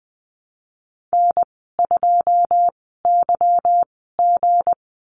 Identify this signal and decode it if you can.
Morse code, heard as a keyed tone, about 20 wpm, 700 Hz. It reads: N 2 Y G
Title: 200 Lesson 32g - Letter k - Call Signs 20wpm